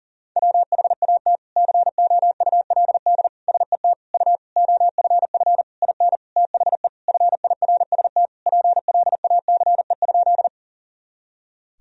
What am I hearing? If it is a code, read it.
Morse code, heard as a keyed tone, about 38 wpm, 700 Hz. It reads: WHAT COULD SET U OFF IN THE FIRST PLACE?
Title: Sentences from Top 200 Words and Abbreviations 40wpm